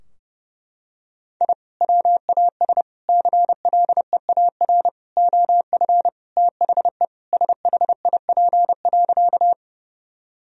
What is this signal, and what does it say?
Morse code, heard as a keyed tone, about 30 wpm, 700 Hz. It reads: I WAS CLEAR OF THE SHIP.
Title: Sentences from Top 1000 Words - Review Part 2 - 30wpm